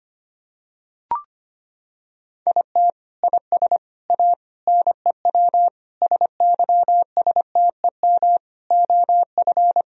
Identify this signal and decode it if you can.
Morse code, heard as a keyed tone, about 25 wpm, 700 Hz. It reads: IT IS A NEW SYSTEM OF
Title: Sentences from Top 1000 Words - Review Part 3 - 25wpm